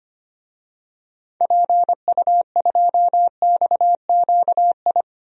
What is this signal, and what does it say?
Morse code, heard as a keyed tone, about 25 wpm, 700 Hz. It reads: PU2XQI
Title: DXCC-100 Call Signs - 6 Characters - 25wpm